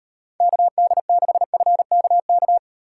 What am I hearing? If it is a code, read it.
Morse code, heard as a keyed tone, about 38 wpm, 700 Hz. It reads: KD6FKK